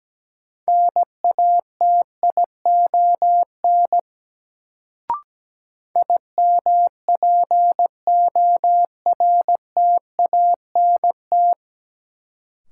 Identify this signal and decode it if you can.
Morse code, heard as a keyed tone, about 17 wpm, 700 Hz. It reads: NATION IMPORTANT
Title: Sets of 3 Words - New Word Drill - 17wpm